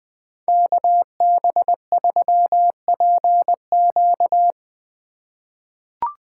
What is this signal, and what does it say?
Morse code, heard as a keyed tone, about 20 wpm, 700 Hz. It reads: KB3PQ
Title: USA Call Signs - 5 Characters - 2x Character Spacing - 20wpm